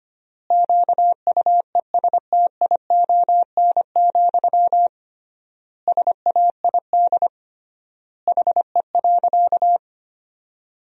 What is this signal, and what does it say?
Morse code, heard as a keyed tone, about 25 wpm, 700 Hz. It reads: QUESTION, SAID HE.
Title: 3x Word-Spacing - Sentences from Top 400 Words - 25wpm